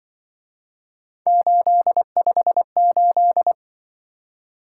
Morse code, heard as a keyed tone, about 24 wpm, 700 Hz. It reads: 858
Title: Speed-Racing - Three Letter-Number Combinations - 20wpm